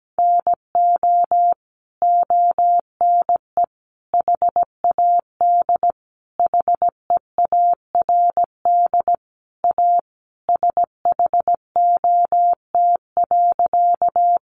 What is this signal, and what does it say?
Morse code, heard as a keyed tone, about 17 wpm, 700 Hz. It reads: NO ONE HAD HEARD A SHOT.
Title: Sentences from Top 1100 Words 17wpm